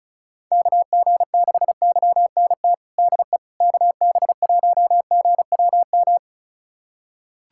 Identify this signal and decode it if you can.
Morse code, heard as a keyed tone, about 34 wpm, 700 Hz. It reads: KG6YNT DE KB1GWM
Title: USA Callsign DE Callsign - 6 Characters - 35wpm